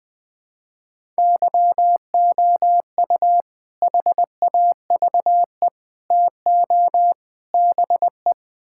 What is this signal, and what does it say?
Morse code, heard as a keyed tone, about 20 wpm, 700 Hz. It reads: YOU HAVE TO BE
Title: Build A Phrase - 4-Word Phrases - 20wpm